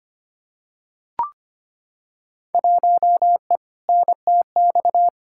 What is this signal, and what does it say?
Morse code, heard as a keyed tone, about 25 wpm, 700 Hz. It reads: 1E NTX
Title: Field Day Exchanges - 25wpm